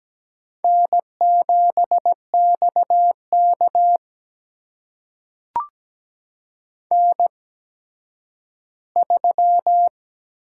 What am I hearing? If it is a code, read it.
Morse code, heard as a keyed tone, about 17 wpm, spaced as for 10 wpm, 700 Hz. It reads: N7XK N3
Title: USA Call Signs - 4 Characters - 8x Character Spacing - 17wpm